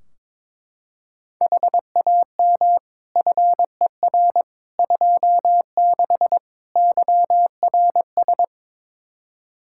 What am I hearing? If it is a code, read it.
Morse code, heard as a keyed tone, about 22 wpm, 700 Hz. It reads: HAM FER 26 YRS
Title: Practice QSO - Exchange 3 - More Info - 22wpm